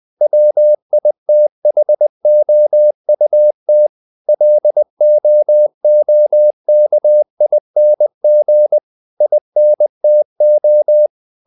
Morse code, heard as a keyed tone, about 20 wpm, 600 Hz. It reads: WITHOUT LOOKING INTO
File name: the old man and the sea / the old man and the sea part 1 20wpm